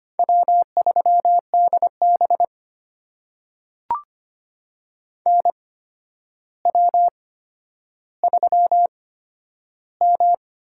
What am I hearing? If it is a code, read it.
Morse code, heard as a keyed tone, about 25 wpm, spaced as for 11 wpm, 700 Hz. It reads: W3DB NW3M
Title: USA Call Signs - 4 Characters - 8x Character Spacing - 25wpm